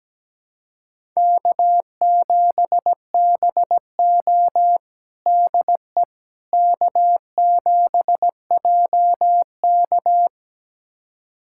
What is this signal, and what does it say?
Morse code, heard as a keyed tone, about 17 wpm, 700 Hz. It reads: K7BO DE K7JK
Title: USA Callsign DE Callsign - 4 Characters - 17wpm